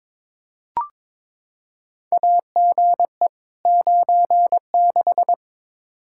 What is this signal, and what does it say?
Morse code, heard as a keyed tone, about 22 wpm, 700 Hz. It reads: AGE 96
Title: Practice QSO - Exchange 3 - More Info - 22wpm